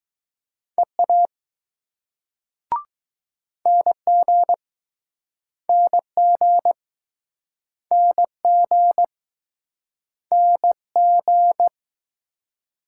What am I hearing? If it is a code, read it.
Morse code, heard as a keyed tone, about 20 wpm, 700 Hz. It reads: EA NG NG NG N G